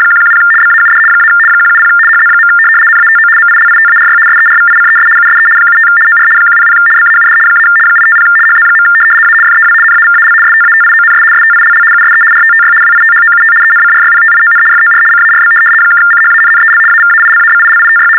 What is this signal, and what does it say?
RTTY, about 75 baud. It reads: 110 BAUD - 170 HZ. NATO MILITARY SERVICES USE 75 OR 100 BAUD - 850 HZ. A FEW NAVAL STATIONS STILL USE RTTY WITHOUT ENCRYPTION FOR CARB (CHANNEL AVAILABILITY
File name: teletype-RTTY-75